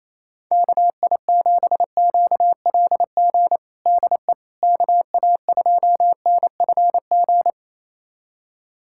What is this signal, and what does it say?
Morse code, heard as a keyed tone, about 28 wpm, 700 Hz. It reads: KI7QLG DE KA2NFG
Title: USA Callsign DE Callsign - 6 Characters - 28wpm